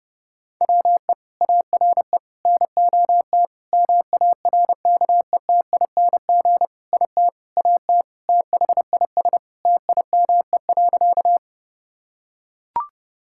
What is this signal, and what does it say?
Morse code, heard as a keyed tone, about 30 wpm, 700 Hz. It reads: WE ARE NOT MARKETING IT AT THIS TIME.
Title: Sentences from Top 1000 Words - Review Part 5 - 30wpm